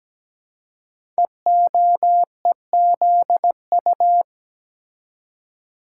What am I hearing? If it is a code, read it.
Morse code, heard as a keyed tone, about 17 wpm, 700 Hz. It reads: EOEZU
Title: Code Groups - 5 Letters - 17wpm